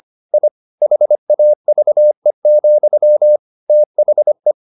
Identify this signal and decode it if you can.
Morse code, heard as a keyed tone, about 25 wpm, 600 Hz. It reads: I HAVE, THE
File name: the old man and the sea part 1 25wpm